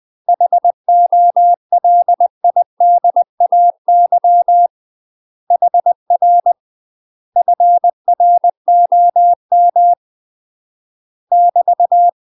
Morse code, heard as a keyed tone, about 20 wpm, 700 Hz. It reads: HOLIDAY HR FROM <BT>
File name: UK MASTER QSOs TRAINING / 20WPM / UK MASTER QSOs 0012